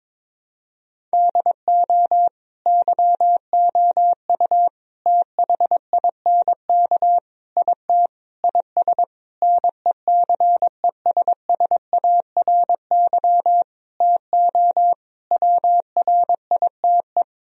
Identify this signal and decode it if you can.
Morse code, heard as a keyed tone, about 22 wpm, 700 Hz. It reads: DO YOU THINK IT IS NECESSARY TO WRITE
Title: Sentences from Top 1000 Words - Review Part 1 - 22wpm